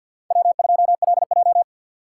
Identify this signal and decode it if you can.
Morse code, heard as a keyed tone, about 51 wpm, 700 Hz. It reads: W2LJ
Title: USA Call Signs - 4 Characters - 3x Character Spacing - 50wpm